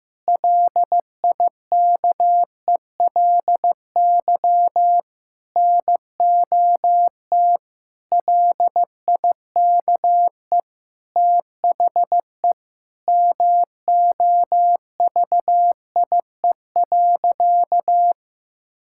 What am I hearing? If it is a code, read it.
Morse code, heard as a keyed tone, about 15 wpm, 700 Hz. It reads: LIKELY NOT LIKE THE MOVIE.
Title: Sentences from Top 1000 Words - Review Part 5 - 15wpm